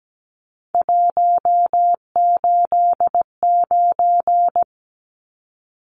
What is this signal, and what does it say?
Morse code, heard as a keyed tone, about 17 wpm, 700 Hz. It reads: 189
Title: Numbers - 3-Digit 17wpm